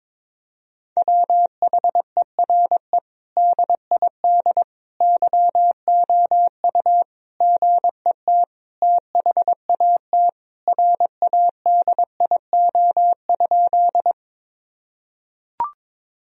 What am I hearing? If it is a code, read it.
Morse code, heard as a keyed tone, about 22 wpm, 700 Hz. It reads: WHERE DID YOU GET THAT RADIO?
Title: Sentences from Top 1000 Words - Review Part 3 - 22wpm